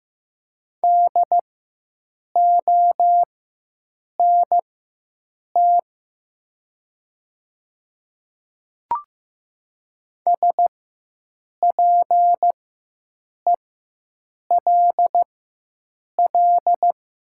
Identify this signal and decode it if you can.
Morse code, heard as a keyed tone, about 15 wpm, spaced as for 7 wpm, 700 Hz. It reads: DONT SPELL
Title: 4x Character Spacing - Top 300 Words - 15wpm